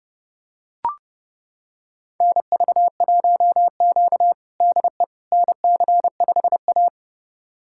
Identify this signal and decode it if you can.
Morse code, heard as a keyed tone, about 30 wpm, 700 Hz. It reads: NV1Q DE NC5A
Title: USA Callsign DE Callsign - 4 Characters - 30wpm